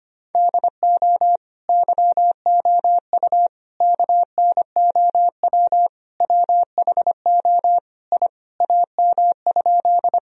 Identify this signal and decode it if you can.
Morse code, heard as a keyed tone, about 25 wpm, 700 Hz. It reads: DO YOU KNOW WHO I AM?
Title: Sentences from Top 1000 Words - Review Part 2 - 25wpm